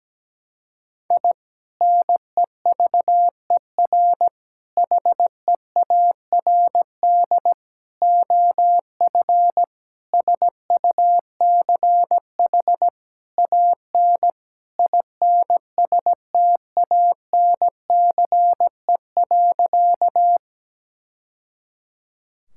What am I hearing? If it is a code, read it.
Morse code, heard as a keyed tone, about 17 wpm, 700 Hz. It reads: I NEVER HEARD OF SUCH AN INSTANCE.
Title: Sentences from Top 1000 Words - Review Part 5 - 17wpm